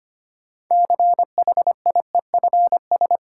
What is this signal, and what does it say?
Morse code, heard as a keyed tone, about 25 wpm, 700 Hz. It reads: CHIEFS
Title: Sets of 5 Words Encore - New Word Drill - 25wpm